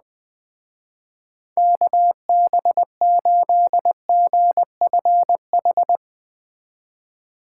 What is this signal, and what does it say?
Morse code, heard as a keyed tone, about 20 wpm, 700 Hz. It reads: KB8GFH
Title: Speed Racing - USA 2x3 Call Signs 17wpm